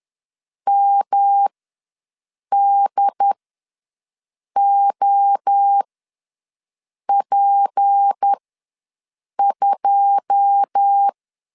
Morse code, heard as a keyed tone, about 10 wpm, spaced as for 6 wpm, 800 Hz. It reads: MDOP2
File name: LEZIONI / 10 SIGNS / 10A SLASH ws 10wpm 4121